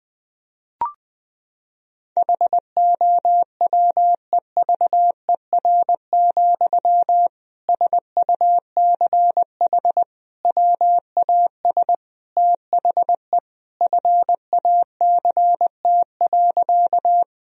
Morse code, heard as a keyed tone, about 20 wpm, 700 Hz. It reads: HOWEVER, SUCH WAS THE FACT.
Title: Sentences from Top 1000 Words - Review Part 3 - 20wpm